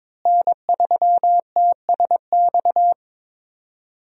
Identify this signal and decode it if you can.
Morse code, heard as a keyed tone, about 22 wpm, 700 Hz. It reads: N3TSX
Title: USA Call Signs - 5 Characters - Repeated 2x - 22wpm